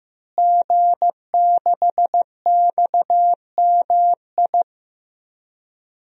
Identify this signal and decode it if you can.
Morse code, heard as a keyed tone, about 16 wpm, 700 Hz. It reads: G6XMI
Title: DXCC-100 Call Signs - 5 Characters - Repeated 2x - 15wpm